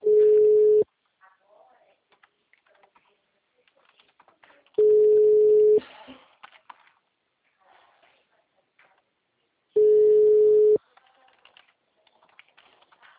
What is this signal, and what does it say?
Ringback tone (Europe, 425 Hz), with speech.
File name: out- -2019- - - .106